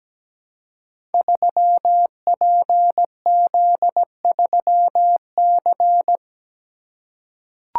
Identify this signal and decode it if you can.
Morse code, heard as a keyed tone, about 17 wpm, 700 Hz. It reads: 3PZ3C
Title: Code Groups - 5 Letter:Numbers - 17wpm